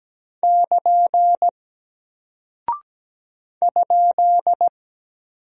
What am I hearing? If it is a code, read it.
Morse code, heard as a keyed tone, about 17 wpm, 700 Hz. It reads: <KN> ?
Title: Single Character (Letter:Number:Punctuation:Prosign) 17wpm